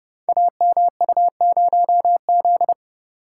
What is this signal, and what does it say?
Morse code, heard as a keyed tone, about 29 wpm, 700 Hz. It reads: AMU0Z
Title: Code Groups - 5 Letter:Numbers - 30wpm